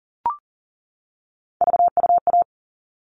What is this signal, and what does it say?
Morse code, heard as a keyed tone, about 40 wpm, 700 Hz. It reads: VUA
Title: All Cut Numbers - 3-Digit 40wpm